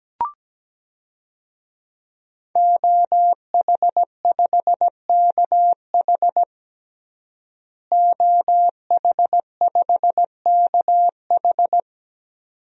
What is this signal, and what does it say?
Morse code, heard as a keyed tone, about 17 wpm, 700 Hz. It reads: OH5KH OH5KH
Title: DXCC-100 Call Signs - 5 Characters - Repeated 2x - 17wpm